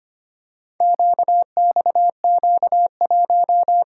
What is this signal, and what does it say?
Morse code, heard as a keyed tone, about 25 wpm, 700 Hz. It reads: QXQ1
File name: practice / 25wpm / Four Letter-Number Combinations - 25wpm